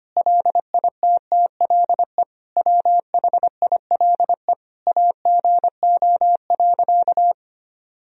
Morse code, heard as a keyed tone, about 25 wpm, 700 Hz. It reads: LITTLE WHILE AGO.
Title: Sentences from Top 1000 Words - Review Part 3 - 25wpm